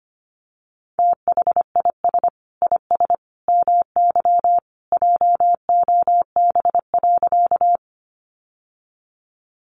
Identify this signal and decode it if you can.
Morse code, heard as a keyed tone, about 25 wpm, 700 Hz. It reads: THIS IS MY JOB.
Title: Sentences from Top 600 Words 25wpm